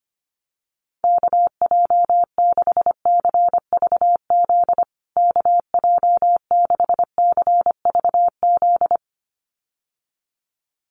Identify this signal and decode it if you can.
Morse code, heard as a keyed tone, about 25 wpm, 700 Hz. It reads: KJ6CVZ KJ6CVZ
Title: USA Call Signs Repeated 2x 25wpm